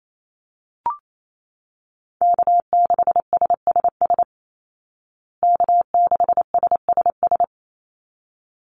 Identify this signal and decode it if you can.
Morse code, heard as a keyed tone, about 28 wpm, 700 Hz. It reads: K6SSS K6SSS
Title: Call Signs Repeated 3x - 28wpm